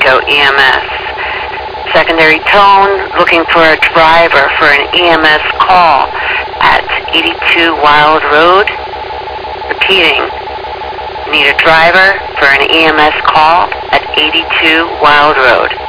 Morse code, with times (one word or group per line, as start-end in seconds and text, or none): none